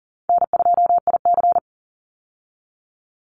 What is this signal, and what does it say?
Morse code, heard as a keyed tone, about 37 wpm, 700 Hz. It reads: N2IC
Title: Call Signs Repeated 3x - 40wpm